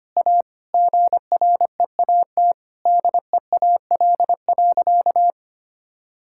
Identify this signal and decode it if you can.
Morse code, heard as a keyed tone, about 25 wpm, 700 Hz. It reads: A GREAT DEAL.
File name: practice / 25wpm / Sentences with 3-Word Phrases - 25wpm